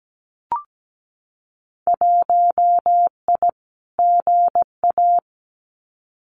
Morse code, heard as a keyed tone, about 17 wpm, 700 Hz. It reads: 1I GA
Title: Winter Field Day Exchanges Encore 17wpm